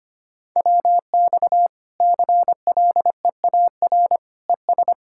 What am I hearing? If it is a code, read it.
Morse code, heard as a keyed tone, about 25 wpm, 700 Hz. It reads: WX CLEAR ES
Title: Practice QSO - Exchange 2 - Info - 25wpm